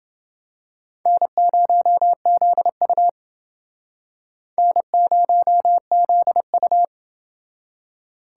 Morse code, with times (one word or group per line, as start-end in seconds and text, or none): 1.1-3.1 N0ZU
4.6-6.8 N0ZU